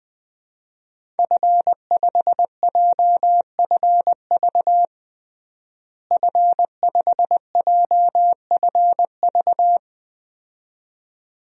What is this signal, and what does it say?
Morse code, heard as a keyed tone, about 20 wpm, 700 Hz. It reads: F5JFV F5JFV
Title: DXCC-100 Call Signs - 5 Characters - Repeated 2x - 20wpm